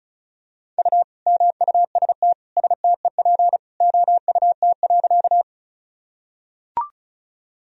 Morse code, heard as a keyed tone, about 34 wpm, 700 Hz. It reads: U MUST STEP OUT.
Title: Sentences from Top 400 Words and Abbreviations 35wpm